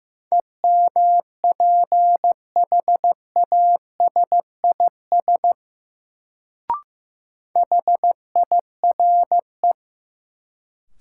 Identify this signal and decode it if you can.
Morse code, heard as a keyed tone, about 15 wpm, 700 Hz. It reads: EMPHASIS HIRE
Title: Sentences from Top 1400 Words - New Word Drill Alt - 15wpm